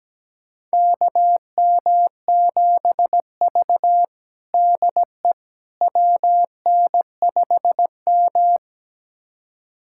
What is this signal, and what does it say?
Morse code, heard as a keyed tone, about 17 wpm, 700 Hz. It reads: KM7V DE WN5M
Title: USA Callsign DE Callsign - 4 Characters - 17wpm